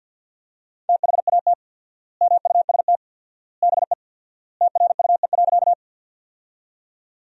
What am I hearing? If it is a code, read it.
Morse code, heard as a keyed tone, about 50 wpm, 700 Hz. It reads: THAT MUST BE TRUE.
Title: 4x Word-Spacing - Sentences from Top 400 Words - 50wpm